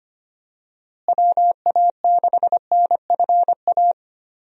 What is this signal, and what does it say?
Morse code, heard as a keyed tone, about 25 wpm, 700 Hz. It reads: WA6NFA
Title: USA Call Signs - 6 Characters - Repeated 2x - 25wpm